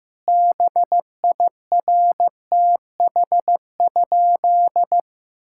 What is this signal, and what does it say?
Morse code, heard as a keyed tone, about 15 wpm, 700 Hz. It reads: BIRTH?
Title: Sentences from Top 1000 Words - Review Part 5 - 15wpm